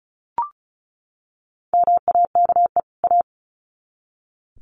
Morse code, heard as a keyed tone, about 34 wpm, 700 Hz. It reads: MAKE A
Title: Sets of 2 Words 35wpm